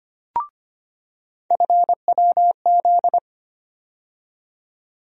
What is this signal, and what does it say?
Morse code, heard as a keyed tone, about 25 wpm, 700 Hz. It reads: FWZ
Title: USA Call Sign Postfixes - 3 Letters 25wpm